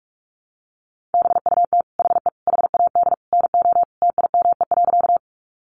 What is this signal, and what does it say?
Morse code, heard as a keyed tone, about 44 wpm, 700 Hz. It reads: BUT HE HAD NO TIME.
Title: Sentences from Top 100 Words 45wpm